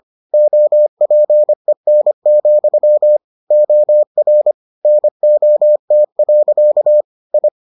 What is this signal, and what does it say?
Morse code, heard as a keyed tone, about 25 wpm, 600 Hz. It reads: OPEN, OR NOT. I